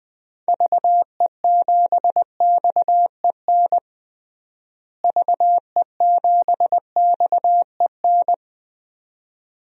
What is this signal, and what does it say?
Morse code, heard as a keyed tone, about 20 wpm, 700 Hz. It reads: VE7XEN VE7XEN
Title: DXCC-100 Call Signs - 6 Characters - Repeated 2x - 20wpm